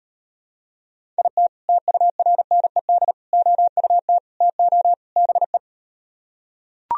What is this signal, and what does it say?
Morse code, heard as a keyed tone, about 38 wpm, 700 Hz. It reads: IT TURNED OUT TO BE